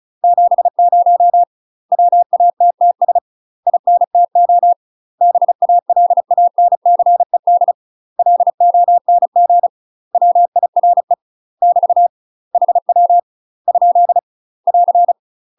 Morse code, heard as a keyed tone, about 34 wpm, 700 Hz. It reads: 70 WATTS INTO BALANCED LONG WIRE <BT> HW ? <AR>
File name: UK MASTER QSOs TRAINING / 35WPM / UK MASTER QSOs 0025